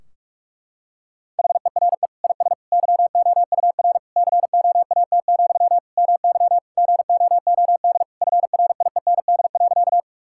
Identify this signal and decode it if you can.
Morse code, heard as a keyed tone, about 41 wpm, 700 Hz. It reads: HERE IS YOUR COAT, MY GOOD FRIEND.
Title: Sentences from Top 1600 Words 45wpm